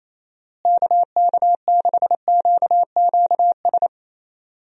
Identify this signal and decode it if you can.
Morse code, heard as a keyed tone, about 27 wpm, 700 Hz. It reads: KK6QQS
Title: USA Call Signs - 6 Characters - Repeated 2x - 28wpm